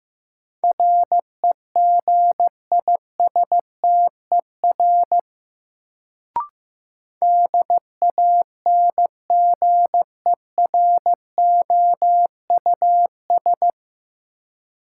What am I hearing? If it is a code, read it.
Morse code, heard as a keyed tone, about 15 wpm, 700 Hz. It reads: REGISTER DANGEROUS
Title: Sentences from Top 1200 Words - New Word Drill - 15wpm